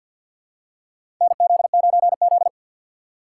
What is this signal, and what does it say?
Morse code, heard as a keyed tone, about 53 wpm, 690 Hz. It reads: NZ9Z
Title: USA Call Signs - 4 Characters - 6x Character Spacing - 50wpm